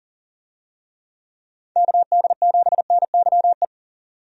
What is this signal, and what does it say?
Morse code, heard as a keyed tone, about 40 wpm, 700 Hz. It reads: KD7NYE
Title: USA Call Signs - 6 Characters - 40wpm